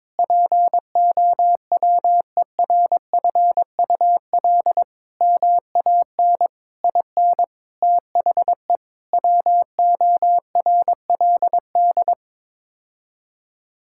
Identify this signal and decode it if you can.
Morse code, heard as a keyed tone, about 22 wpm, 700 Hz. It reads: POWERFUL MAN IN THE WORLD
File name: practice / 22wpm / Sets of 7 Words - 22wpm